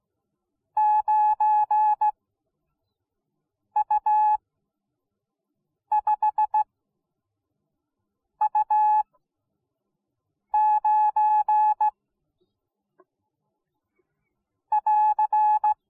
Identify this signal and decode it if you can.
Morse code, heard as a keyed tone, about 17 wpm, spaced as for 5 wpm, 850 Hz. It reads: 9U5U9 <AR>